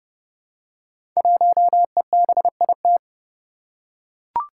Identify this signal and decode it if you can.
Morse code, heard as a keyed tone, about 30 wpm, 700 Hz. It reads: 1EBIT
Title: Five Challenging Letter-Number Combinations 30wpm